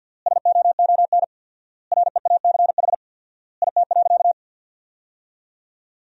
Morse code, heard as a keyed tone, about 50 wpm, 700 Hz. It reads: SOON REACH IT.
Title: 4x Word-Spacing - Sentences from Top 400 Words - 50wpm